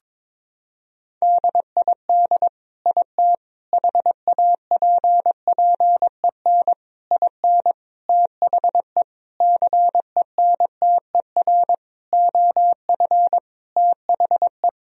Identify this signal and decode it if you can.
Morse code, heard as a keyed tone, about 22 wpm, 700 Hz. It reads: DID IT HAPPEN IN THE CENTER OF THE
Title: Sentences from Top 1000 Words - Review Part 2 - 22wpm